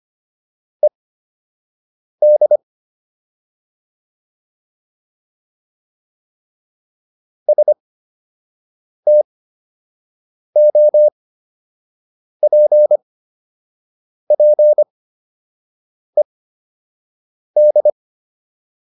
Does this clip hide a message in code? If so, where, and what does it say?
Yes: Morse code from 0.8 to 17.9 s, heard as a keyed tone, about 25 wpm, spaced as for 6 wpm, 600 Hz. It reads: ED STOPPED